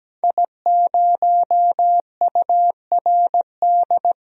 Morse code, heard as a keyed tone, about 17 wpm, 700 Hz. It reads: I0URD
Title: DXCC-100 Call Signs - 5 Characters - Repeated 3x - 17wpm